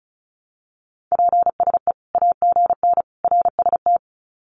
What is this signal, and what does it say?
Morse code, heard as a keyed tone, about 34 wpm, 700 Hz. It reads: PSE AGN RST
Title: Common QSO Phrases 35wpm